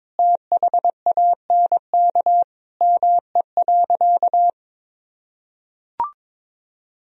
Morse code, heard as a keyed tone, about 22 wpm, 700 Hz. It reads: THANK ME.
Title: Sentences from Top 1000 Words - Review Part 1 - 22wpm